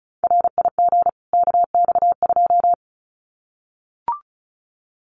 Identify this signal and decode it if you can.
Morse code, heard as a keyed tone, about 34 wpm, 700 Hz. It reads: RIG KX2
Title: Common QSO Phrases 35wpm